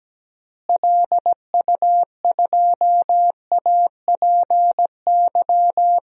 Morse code, heard as a keyed tone, about 17 wpm, 700 Hz. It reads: LU2APY
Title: DXCC-100 Call Signs - 6 Characters - Repeated 2x - 17wpm